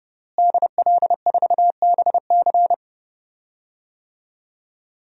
Morse code, heard as a keyed tone, about 30 wpm, 700 Hz. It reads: DL4BC
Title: DXCC-100 Call Signs - 5 Characters - Repeated 2x - 30wpm